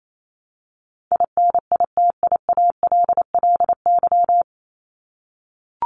Morse code, heard as a keyed tone, about 28 wpm, 700 Hz. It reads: INITIALLY
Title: Top 1251-1500 Words 28wpm